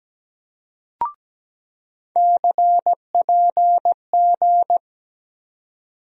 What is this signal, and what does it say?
Morse code, heard as a keyed tone, about 17 wpm, 700 Hz. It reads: CPG
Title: USA Call Sign Postfixes - 3 Letters 17wpm